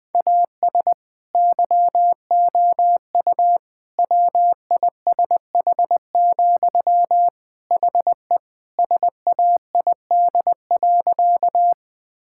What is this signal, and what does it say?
Morse code, heard as a keyed tone, about 21 wpm, 700 Hz. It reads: AS YOU WISH, HE SAID.
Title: Sentences from Top 1600 Words 20wpm